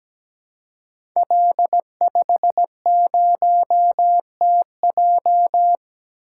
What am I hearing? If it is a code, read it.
Morse code, heard as a keyed tone, about 17 wpm, 700 Hz. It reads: L50TJ
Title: Code Groups - 5 Letter:Numbers - 17wpm